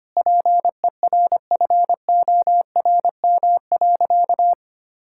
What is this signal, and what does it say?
Morse code, heard as a keyed tone, about 25 wpm, 700 Hz. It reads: PERFORM.
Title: Sentences from Top 1000 Words - Review Part 5 - 25wpm